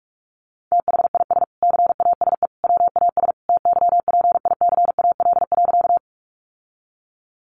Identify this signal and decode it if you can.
Morse code, heard as a keyed tone, about 44 wpm, 700 Hz. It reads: THIS CASE WAS TYPICAL.
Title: Sentences from Top 1100 Words 45wpm